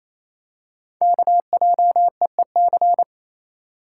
Morse code, heard as a keyed tone, about 28 wpm, 700 Hz. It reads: KJEEC
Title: Code Groups - 5 Letters - 28wpm